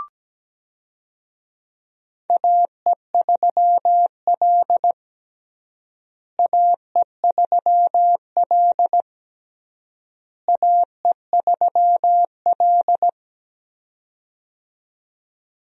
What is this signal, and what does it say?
Morse code, heard as a keyed tone, about 17 wpm, 700 Hz. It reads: AE3L AE3L AE3L
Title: USA Call Signs - 4 Characters - Repeated 3x - 17wpm